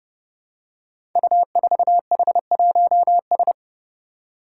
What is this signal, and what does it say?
Morse code, heard as a keyed tone, about 31 wpm, 700 Hz. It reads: U4H1S
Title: Five Challenging Letter-Number Combinations 30wpm